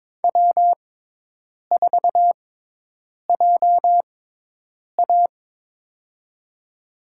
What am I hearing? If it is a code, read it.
Morse code, heard as a keyed tone, about 22 wpm, spaced as for 8 wpm, 700 Hz. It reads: W4JA
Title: USA Call Signs - 4 Characters - 6x Character Spacing - 22wpm